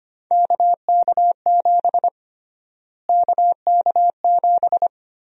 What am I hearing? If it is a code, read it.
Morse code, heard as a keyed tone, about 25 wpm, 700 Hz. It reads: KK7 KK7
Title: USA Call Sign Prefixes - Repeated 2x - 25wpm